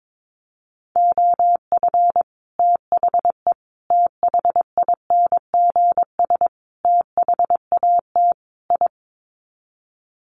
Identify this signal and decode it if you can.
Morse code, heard as a keyed tone, about 22 wpm, 700 Hz. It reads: OF THE THINGS THAT I
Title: Sets of 5 Words Encore 22wpm